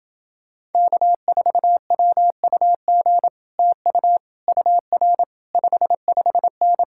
Morse code, heard as a keyed tone, about 30 wpm, 700 Hz. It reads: K4WUG TU UR 55N